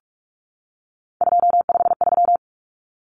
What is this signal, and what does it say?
Morse code, heard as a keyed tone, about 44 wpm, 700 Hz. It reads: 253
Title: Numbers - 3-Digit 45wpm